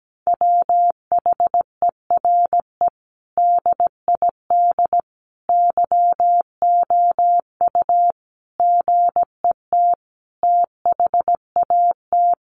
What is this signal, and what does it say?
Morse code, heard as a keyed tone, about 17 wpm, 700 Hz. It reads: WHERE DID YOU GET THAT
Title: Sentences from Top 1200 Words 17wpm